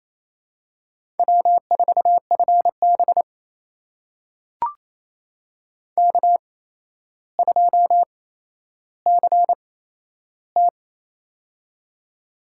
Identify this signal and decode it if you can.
Morse code, heard as a keyed tone, about 27 wpm, spaced as for 12 wpm, 700 Hz. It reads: W4FB K2CT
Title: USA Call Signs - 4 Characters - 8x Character Spacing - 28wpm